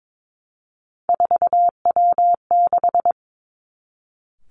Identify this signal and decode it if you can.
Morse code, heard as a keyed tone, about 22 wpm, 700 Hz. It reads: 4W6
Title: Three Challenging Letter-Number Combinations 22wpm